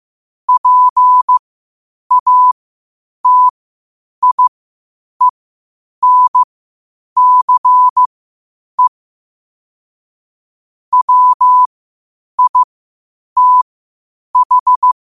Morse code, heard as a keyed tone, about 15 wpm, spaced as for 8 wpm, 1000 Hz. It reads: PATIENCE WITH
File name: qotdfiles / 8wpm